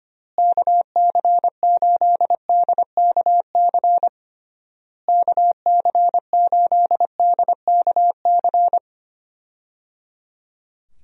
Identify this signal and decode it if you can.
Morse code, heard as a keyed tone, about 25 wpm, 700 Hz. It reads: KC8DKC KC8DKC
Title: USA Call Signs - 6 Characters - Repeated 3x - 25wpm